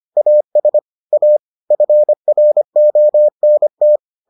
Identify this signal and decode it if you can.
Morse code, heard as a keyed tone, about 25 wpm, 600 Hz. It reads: AS A FRONT